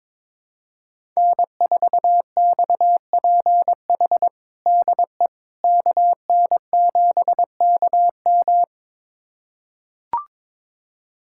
Morse code, heard as a keyed tone, about 22 wpm, 700 Hz. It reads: N4XPH DE KN7KM
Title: USA Callsign DE Callsign - 5 Characters - 22wpm